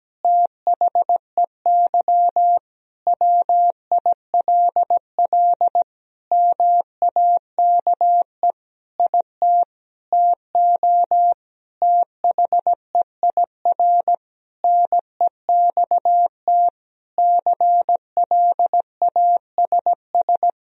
Morse code, heard as a keyed tone, about 17 wpm, 700 Hz. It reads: THEY WILL MAKE IT TO THEIR NEXT CLASS
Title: Sentences from Top 1000 Words - Review Part 2 - 17wpm